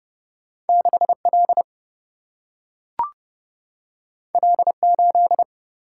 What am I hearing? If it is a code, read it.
Morse code, heard as a keyed tone, about 30 wpm, 700 Hz. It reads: 6L L8
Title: Single-Letter Single-Number 30wpm